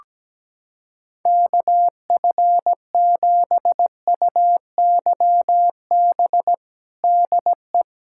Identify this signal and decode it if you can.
Morse code, heard as a keyed tone, about 17 wpm, 700 Hz. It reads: KF7UYB DE
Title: USA Callsign DE Callsign - 6 Characters - 17wpm